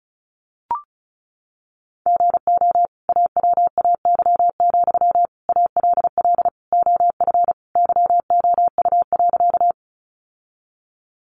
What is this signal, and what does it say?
Morse code, heard as a keyed tone, about 34 wpm, 700 Hz. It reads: GO AWAY, ALL OF YOU.
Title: Sentences from Top 600 Words 35wpm